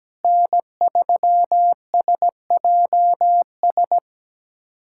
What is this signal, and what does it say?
Morse code, heard as a keyed tone, about 17 wpm, 700 Hz. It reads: N3SJS
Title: USA Callsign DE Callsign 17wpm